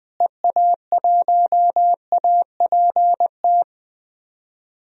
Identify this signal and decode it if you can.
Morse code, heard as a keyed tone, about 20 wpm, 700 Hz. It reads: EA1APT
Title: DXCC-100 Call Signs - 6 Characters - Repeated 3x - 20wpm